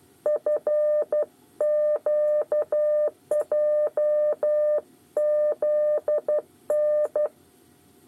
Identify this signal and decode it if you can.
Morse code, heard as a keyed tone, about 12 wpm, 600 Hz. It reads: FQJZN
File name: 8 bis lezione cw 2